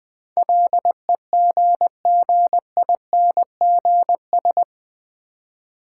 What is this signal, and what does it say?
Morse code, heard as a keyed tone, about 20 wpm, 700 Hz. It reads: LEGGINGS
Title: Difficult Words with Hints - 20wpm